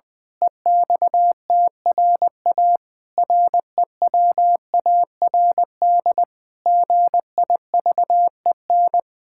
Morse code, heard as a keyed tone, about 20 wpm, 700 Hz. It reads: EXTRA REWARD GIVEN
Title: Pangram Challenge v2 - 20wpm